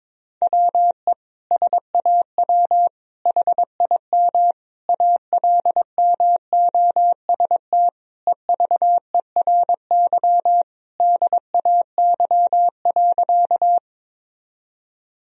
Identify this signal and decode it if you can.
Morse code, heard as a keyed tone, about 21 wpm, 700 Hz. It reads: WE SAW HIM ALMOST EVERY DAY.
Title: Sentences from Top 1000 Words - Review Part 4 - 22wpm